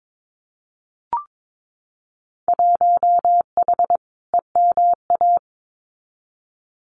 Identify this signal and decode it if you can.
Morse code, heard as a keyed tone, about 22 wpm, 700 Hz. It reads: 1H EMA
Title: Winter Field Day Exchanges 22wpm